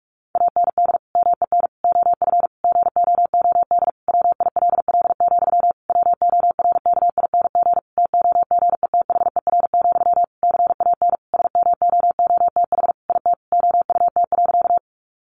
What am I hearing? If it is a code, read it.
Morse code, heard as a keyed tone, about 44 wpm, 700 Hz. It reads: AND MEN OF GOOD WILL, WORKING TOGETHER, CAN SMOOTH IT OUT.
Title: Sentences from Top 1400 Words 45wpm